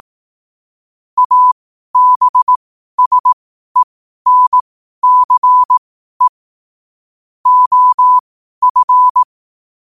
Morse code, heard as a keyed tone, about 18 wpm, spaced as for 12 wpm, 1000 Hz. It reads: ABSENCE OF
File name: qotdfiles / 12wpm